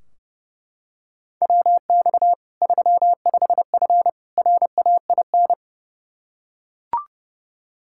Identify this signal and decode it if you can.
Morse code, heard as a keyed tone, about 29 wpm, 700 Hz. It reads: WX 35F RAIN
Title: K1USN SST QSO Practice 30wpm